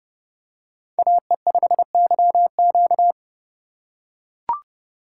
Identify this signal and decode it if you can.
Morse code, heard as a keyed tone, about 30 wpm, 700 Hz. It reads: AE5YQ